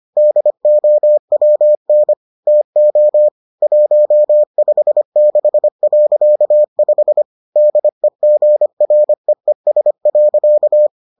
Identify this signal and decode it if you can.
Morse code, heard as a keyed tone, about 25 wpm, 600 Hz. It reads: DOWN TO 156.5 DEGREES.